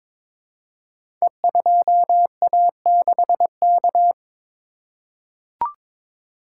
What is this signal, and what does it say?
Morse code, heard as a keyed tone, about 22 wpm, 700 Hz. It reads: E2A6K
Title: Code Groups - 5 Letter:Numbers - 22wpm